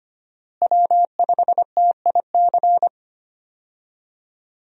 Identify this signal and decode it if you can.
Morse code, heard as a keyed tone, about 25 wpm, 700 Hz. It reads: W5TIC
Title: USA Call Signs - 5 Characters - Repeated 2x - 25wpm